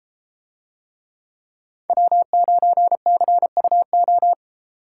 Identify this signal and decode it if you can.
Morse code, heard as a keyed tone, about 32 wpm, 700 Hz. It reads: W9CUO